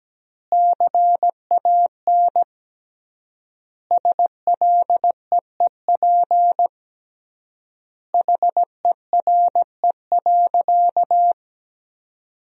Morse code, heard as a keyed tone, about 17 wpm, 700 Hz. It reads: CAN SLEEP HERE.
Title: 3x Word-Spacing - Sentences from Top 500 Words - 17wpm